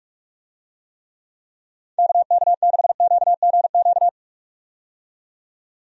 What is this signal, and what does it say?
Morse code, heard as a keyed tone, about 46 wpm, 700 Hz. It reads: KK6QGQ
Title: USA Call Signs - 6 Characters - 45wpm